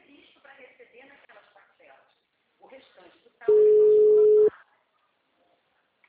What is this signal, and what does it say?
Ringback tone (Europe, 425 Hz), with speech.